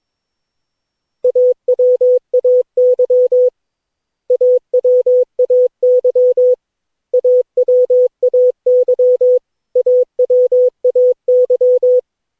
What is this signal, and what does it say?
Morse code, heard as a keyed tone, about 22 wpm, 500 Hz. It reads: AWAY AWAY AWAY AWAY
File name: cw / Session9 cw training